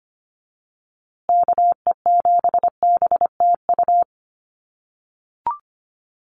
Morse code, heard as a keyed tone, about 25 wpm, 700 Hz. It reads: KE7BTU
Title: USA Call Signs Repeated 2x 25wpm